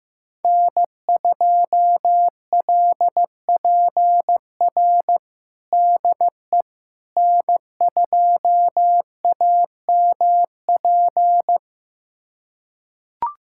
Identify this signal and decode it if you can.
Morse code, heard as a keyed tone, about 15 wpm, 700 Hz. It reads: N2LPR DE N2AMP
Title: USA Callsign DE Callsign - 5 Characters - 15wpm